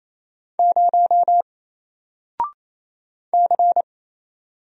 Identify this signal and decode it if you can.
Morse code, heard as a keyed tone, about 29 wpm, 700 Hz. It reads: 0 C
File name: Single-Letter-Number-28wpm